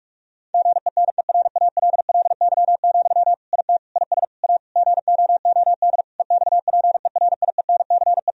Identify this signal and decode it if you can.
Morse code, heard as a keyed tone, about 41 wpm, 700 Hz. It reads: GENERALLY, IT IS A GOOD EXPERIENCE
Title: Sentences from Top 1000 Words - Review Part 4 - 45wpm